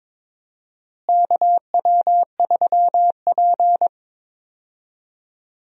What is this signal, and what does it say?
Morse code, heard as a keyed tone, about 22 wpm, 700 Hz. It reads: KW3P
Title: USA Call Signs - 4 Characters - Repeated 3x - 22wpm